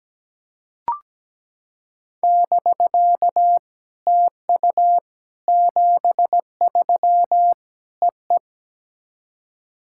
Morse code, heard as a keyed tone, about 17 wpm, 700 Hz. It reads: <BK> TU 73 EE
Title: SOTA QSO Practice 17wpm